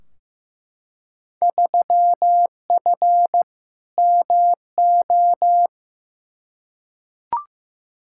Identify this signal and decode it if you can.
Morse code, heard as a keyed tone, about 15 wpm, 700 Hz. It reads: 3F MO
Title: Field Day Exchanges Encore - 15wpm